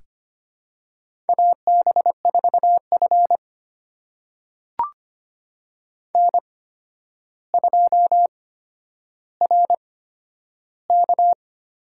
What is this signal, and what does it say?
Morse code, heard as a keyed tone, about 25 wpm, spaced as for 11 wpm, 700 Hz. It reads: AB4F N2RK
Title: USA Call Signs - 4 Characters - 8x Character Spacing - 25wpm